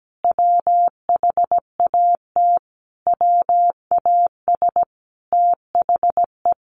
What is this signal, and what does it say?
Morse code, heard as a keyed tone, about 17 wpm, 700 Hz. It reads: WHAT WAS THE
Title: Sentences from Top 1500 Words 17wpm